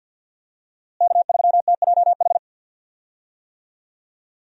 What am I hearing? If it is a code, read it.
Morse code, heard as a keyed tone, about 51 wpm, 700 Hz. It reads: K3TJH
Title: USA Call Signs - 5 Characters - Repeated 3x - 50wpm